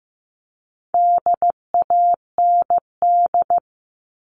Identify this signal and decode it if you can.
Morse code, heard as a keyed tone, about 15 wpm, 700 Hz. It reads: DAND
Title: All Cut Numbers - 4-Digit 15wpm